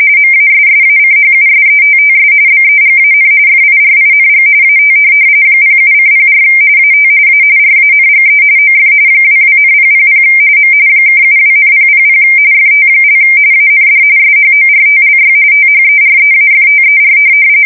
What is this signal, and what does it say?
RTTY, about 45.45 baud. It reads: MASSACHUSETTS BEGAN FIRING AT 07:04 AT A RANGE OF 22,000 M (72,000 FT) AND SHORTLY THEREAFTER
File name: fsk170 / USS-Massachusetts